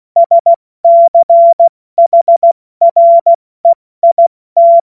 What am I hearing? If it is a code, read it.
Morse code, heard as a keyed tone, about 16 wpm, 680 Hz. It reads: SCHREIT
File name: Morsekurs / Witze / Witz 08. August 2023 70 F80